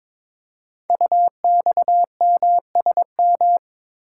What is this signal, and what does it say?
Morse code, heard as a keyed tone, about 22 wpm, 700 Hz. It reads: UXMSM
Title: Code Groups - 5 Letters - 22wpm